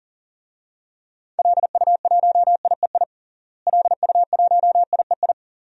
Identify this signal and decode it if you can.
Morse code, heard as a keyed tone, about 40 wpm, 700 Hz. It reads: LU1IEI LU1IEI
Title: DXCC-100 Call Signs - 6 Characters - Repeated 2x - 40wpm